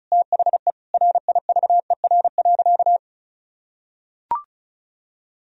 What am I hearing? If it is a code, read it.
Morse code, heard as a keyed tone, about 34 wpm, 700 Hz. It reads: THE RIVER.
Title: Sentences from Top 1000 Words - Review Part 1 - 35wpm